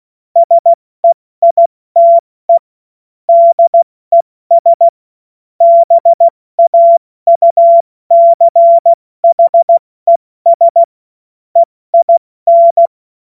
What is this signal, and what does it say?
Morse code, heard as a keyed tone, about 16 wpm, 680 Hz. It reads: SEITE DES BAUCHES EIN
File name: Morsekurs / Witze / Witz 27. Januar 2026 70 F80